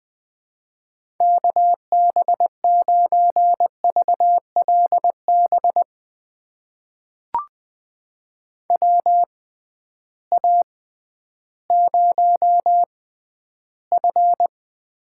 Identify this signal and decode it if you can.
Morse code, heard as a keyed tone, about 20 wpm, spaced as for 12 wpm, 700 Hz. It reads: KB9VLB WA0F
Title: USA Call Signs - 6x Character Spacing - 20wpm